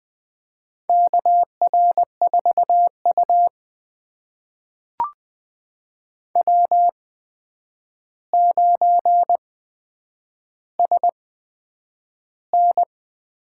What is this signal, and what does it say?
Morse code, heard as a keyed tone, about 20 wpm, spaced as for 9 wpm, 700 Hz. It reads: KR4U W9SN
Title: USA Call Signs - 4 Characters - 8x Character Spacing - 20wpm